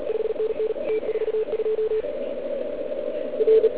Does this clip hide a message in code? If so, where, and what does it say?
Yes: Morse code from 0.1 to 3.7 s, heard as a keyed tone, about 36 wpm, 440 Hz, with noise. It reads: 5NN T42 R